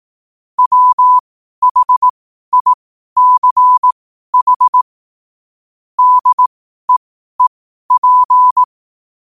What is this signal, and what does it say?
Morse code, heard as a keyed tone, about 18 wpm, spaced as for 12 wpm, 1000 Hz. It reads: WHICH DEEP
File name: qotdfiles / 12wpm